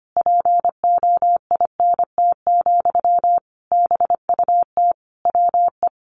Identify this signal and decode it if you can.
Morse code, heard as a keyed tone, about 25 wpm, 700 Hz. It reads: POINT, BUT WE
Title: Sentences from Top 1500 Words 25wpm